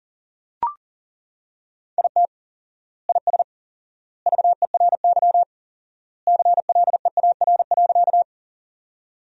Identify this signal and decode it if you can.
Morse code, heard as a keyed tone, about 39 wpm, 700 Hz. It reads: IT IS VERY CLEAR.
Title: 4x Word-Spacing - Sentences from Top 500 Words - 40wpm